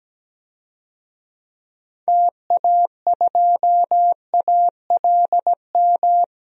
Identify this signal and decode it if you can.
Morse code, heard as a keyed tone, about 17 wpm, 700 Hz. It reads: TA2ALM
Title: DXCC-100 Call Signs - 6 Characters - 17wpm